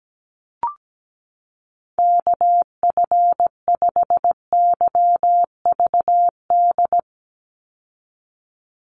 Morse code, heard as a keyed tone, about 17 wpm, 700 Hz. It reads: KF5YVD
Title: USA Call Signs 17wpm